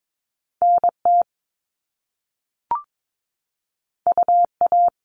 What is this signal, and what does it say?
Morse code, heard as a keyed tone, about 22 wpm, 700 Hz. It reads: NT UA
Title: Top Two Letter Combinations in Words 22wpm